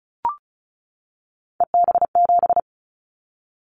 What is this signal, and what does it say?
Morse code, heard as a keyed tone, about 36 wpm, 700 Hz. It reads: EB7
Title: Callsign Prefixes for Top 25 DXCC - 35wpm